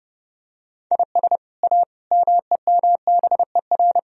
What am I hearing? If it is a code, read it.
Morse code, heard as a keyed tone, about 30 wpm, 700 Hz. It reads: IS A MEMBER